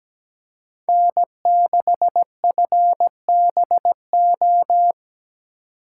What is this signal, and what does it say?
Morse code, heard as a keyed tone, about 17 wpm, 700 Hz. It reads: N6FBO
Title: USA Call Signs - 5 Characters - 8x Character Spacing - 17wpm